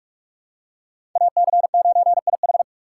Morse code, heard as a keyed tone, about 41 wpm, 700 Hz. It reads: AC9IH
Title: USA Call Signs - 5 Characters - 3x Character Spacing - 45wpm